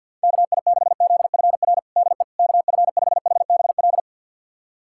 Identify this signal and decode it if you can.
Morse code, heard as a keyed tone, about 51 wpm, 700 Hz. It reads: KI6ZFR DE KF5HBL
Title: USA Callsign DE Callsign - 6 Characters - 50wpm